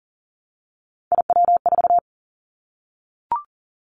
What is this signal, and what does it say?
Morse code, heard as a keyed tone, about 42 wpm, 700 Hz. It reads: IW4
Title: Callsign Prefixes for Top 50 DXCC - 40wpm